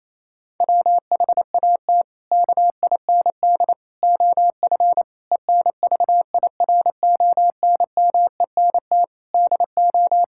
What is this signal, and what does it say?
Morse code, heard as a keyed tone, about 28 wpm, 700 Hz. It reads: WHAT KIND OF ENVIRONMENT DO
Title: Sentences from Top 1000 Words - Review Part 5 - 28wpm